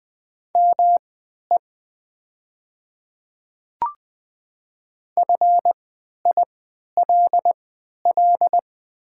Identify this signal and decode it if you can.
Morse code, heard as a keyed tone, about 20 wpm, spaced as for 11 wpm, 700 Hz. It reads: ME FILL
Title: 3x Character Spacing - Top 500 Words - 20wpm